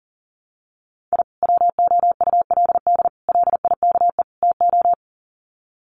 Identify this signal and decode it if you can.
Morse code, heard as a keyed tone, about 40 wpm, 700 Hz. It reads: I WOULD LIKE TO
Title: Sets of 4 Words 40wpm